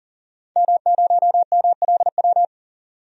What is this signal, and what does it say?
Morse code, heard as a keyed tone, about 40 wpm, 700 Hz. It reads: M0MLW
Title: DXCC-100 Call Signs - 5 Characters - 40wpm